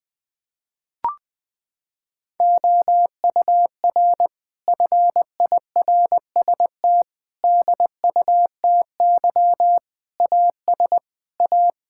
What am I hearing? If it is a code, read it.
Morse code, heard as a keyed tone, about 21 wpm, 700 Hz. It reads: OUR FIRST DUTY AS A
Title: Sentences from Top 1600 Words 20wpm